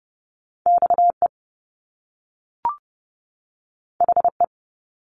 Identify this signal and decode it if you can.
Morse code, heard as a keyed tone, about 30 wpm, 700 Hz. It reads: XE HE
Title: Top Two Letter Combinations in Words 30wpm